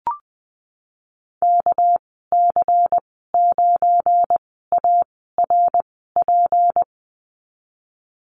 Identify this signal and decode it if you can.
Morse code, heard as a keyed tone, about 20 wpm, spaced as for 15 wpm, 700 Hz. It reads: KC9ARP
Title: 231 Lesson 37g - Letter x - Call Signs 20wpm